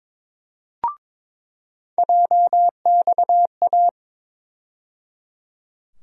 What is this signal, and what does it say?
Morse code, heard as a keyed tone, about 22 wpm, 700 Hz. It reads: JXA
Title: USA Call Sign Postfixes - 3 Letters 22wpm